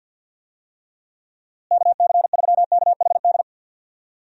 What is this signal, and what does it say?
Morse code, heard as a keyed tone, about 52 wpm, 700 Hz. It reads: KC3KHD
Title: USA Call Signs - 6 Characters - Repeated 2x - 50wpm